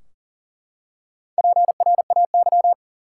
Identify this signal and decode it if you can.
Morse code, heard as a keyed tone, about 40 wpm, 700 Hz. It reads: PRAY
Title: Four Letter Words - 40wpm